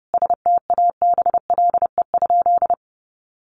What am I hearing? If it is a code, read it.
Morse code, heard as a keyed tone, about 30 wpm, 700 Hz. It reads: STABLE?
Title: Sentences from Top 1500 Words 30wpm